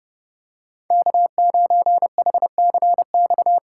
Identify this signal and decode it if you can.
Morse code, heard as a keyed tone, about 31 wpm, 700 Hz. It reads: K9HCX
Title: USA Call Signs - 3x Character Spacing - 30wpm